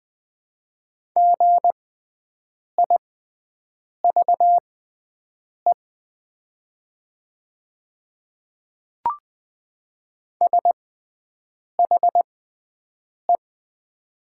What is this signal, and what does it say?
Morse code, heard as a keyed tone, about 20 wpm, spaced as for 7 wpm, 700 Hz. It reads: GIVE SHE
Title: 6x Character Spacing - Top 300 Words - 20wpm